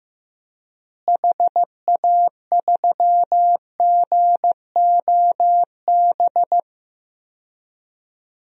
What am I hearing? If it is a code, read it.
Morse code, heard as a keyed tone, about 15 wpm, 700 Hz. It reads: HA3GOB
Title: DXCC-100 Call Signs - 6 Characters - Repeated 3x - 15wpm